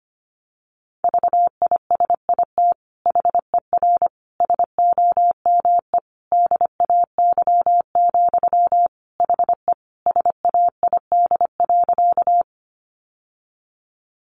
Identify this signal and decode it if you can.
Morse code, heard as a keyed tone, about 25 wpm, 700 Hz. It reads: VISIT HER SOME DAY, HE SAID.
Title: Sentences from Top 900 Words 25wpm